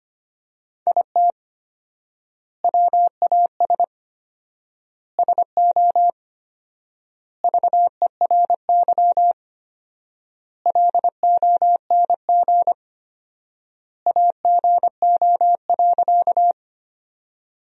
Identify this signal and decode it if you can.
Morse code, heard as a keyed tone, about 25 wpm, 700 Hz. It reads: IT WAS SO VERY LONG AGO.
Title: 4x Word-Spacing - Sentences from Top 500 Words - 25wpm